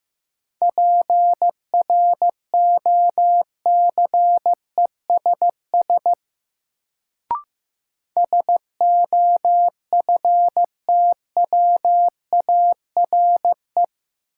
Morse code, heard as a keyed tone, about 15 wpm, 700 Hz. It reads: PROCESS SOFTWARE
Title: Sentences from Top 1000 Words - Review Part 3 - New Word Drill - 15wpm